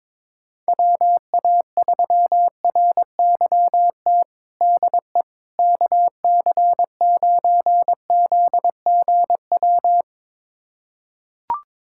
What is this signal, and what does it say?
Morse code, heard as a keyed tone, about 22 wpm, 700 Hz. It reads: WA3RYT DE KC9ZGW
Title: USA Callsign DE Callsign 22wpm